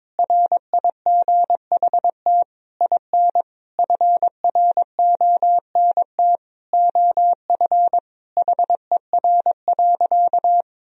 Morse code, heard as a keyed tone, about 22 wpm, 700 Hz. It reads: RIGHT IN FRONT OF HER.
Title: Sentences from Top 1000 Words - Review Part 2 - 22wpm